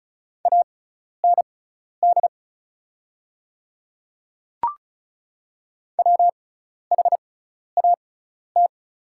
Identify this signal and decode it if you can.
Morse code, heard as a keyed tone, about 36 wpm, spaced as for 12 wpm, 690 Hz. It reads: AND WHAT
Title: 6x Character Spacing - Top 100 Words - 35wpm